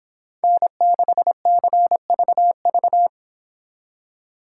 Morse code, heard as a keyed tone, about 29 wpm, 700 Hz. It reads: N6CVV